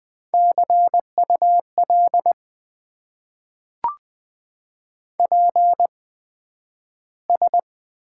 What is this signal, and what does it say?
Morse code, heard as a keyed tone, about 20 wpm, spaced as for 11 wpm, 700 Hz. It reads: CUL PS
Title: Top 90 QSO Elements - 8x Character Spacing - 20wpm